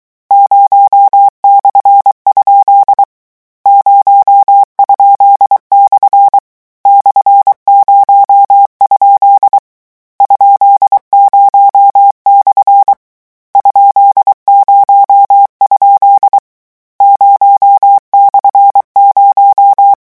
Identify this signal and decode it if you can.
Morse code, heard as a keyed tone, about 23 wpm, 800 Hz. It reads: 0/? 0?/ /0? ?0/ ?0? 0/0
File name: morse mp3 / lecon13-vitesse 22